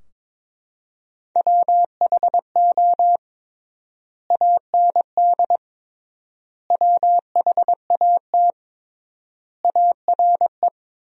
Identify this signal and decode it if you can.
Morse code, heard as a keyed tone, about 22 wpm, 700 Hz. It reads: WHO AND WHAT ARE
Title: 3x Word-Spacing - Sentences from Top 100 Words - 22wpm